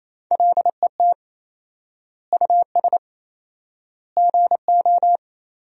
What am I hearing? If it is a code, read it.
Morse code, heard as a keyed tone, about 28 wpm, 700 Hz. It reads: LET US GO
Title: 4x Word-Spacing - Sentences from Top 300 Words - 28wpm